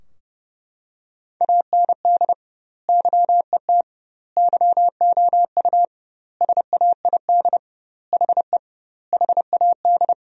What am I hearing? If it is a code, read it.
Morse code, heard as a keyed tone, about 30 wpm, 700 Hz. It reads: AND YET YOU SAID HE HAD
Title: 2x Word-Spacing - Sentences from Top 500 Words - 30wpm